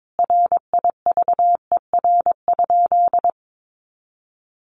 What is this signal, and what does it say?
Morse code, heard as a keyed tone, about 22 wpm, 700 Hz. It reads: RIVER?
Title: Sentences from Top 1400 Words 22wpm